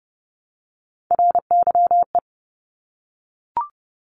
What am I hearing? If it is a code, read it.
Morse code, heard as a keyed tone, about 30 wpm, 700 Hz. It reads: RYE
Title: Multi-letter Phonemes Repeated 3x 30wpm